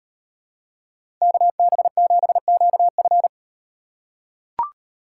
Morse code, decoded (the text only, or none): KB7QF